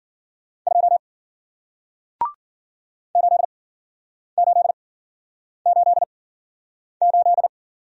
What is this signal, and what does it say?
Morse code, heard as a keyed tone, about 54 wpm, spaced as for 9 wpm, 700 Hz. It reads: 2 8888